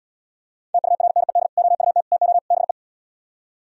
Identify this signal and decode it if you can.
Morse code, heard as a keyed tone, about 77 wpm, 700 Hz. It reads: TURNED OUT TO BE